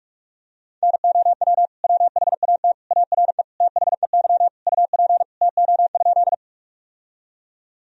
Morse code, heard as a keyed tone, about 45 wpm, 700 Hz. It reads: NOW WHAT ARE THEY UP TO?
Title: Sentences from Top 1000 Words - Review Part 1 - 45wpm